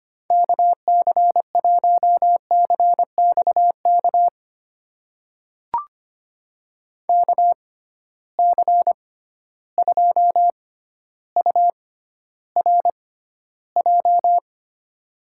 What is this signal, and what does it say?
Morse code, heard as a keyed tone, about 25 wpm, spaced as for 13 wpm, 700 Hz. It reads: KC1CXK KC2URJ
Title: USA Call Signs - 6 Characters - 6x Character Spacing - 25wpm